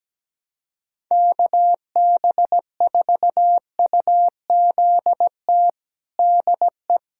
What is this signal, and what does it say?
Morse code, heard as a keyed tone, about 17 wpm, 700 Hz. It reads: KB4UZT DE
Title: USA Callsign DE Callsign - 6 Characters - 17wpm